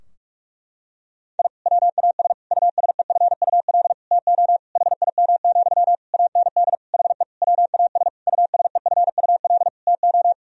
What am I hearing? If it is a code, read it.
Morse code, heard as a keyed tone, about 41 wpm, 700 Hz. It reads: I WAS USEFUL TO HIM, AND HE WAS USEFUL TO
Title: Sentences from Top 1000 Words - Review Part 4 - 45wpm